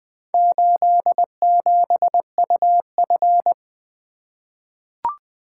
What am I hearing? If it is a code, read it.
Morse code, heard as a keyed tone, about 20 wpm, 700 Hz. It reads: 87UF
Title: Four Letter-Number Combinations - 20wpm